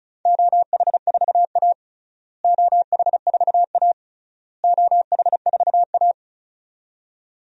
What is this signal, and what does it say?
Morse code, heard as a keyed tone, about 36 wpm, 700 Hz. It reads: OH4A OH4A OH4A
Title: DXCC-100 Call Signs - 4 Characters - Repeated 3x - 35wpm